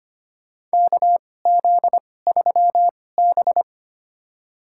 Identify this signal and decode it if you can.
Morse code, heard as a keyed tone, about 25 wpm, spaced as for 18 wpm, 700 Hz. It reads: KZ3B
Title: USA Call Signs - 4 Characters - 2x Character Spacing - 25wpm